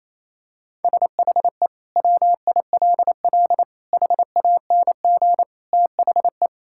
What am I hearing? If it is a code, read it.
Morse code, heard as a keyed tone, about 28 wpm, 700 Hz. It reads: SHE WILL HANG THE
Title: Sentences from Top 1600 Words 28wpm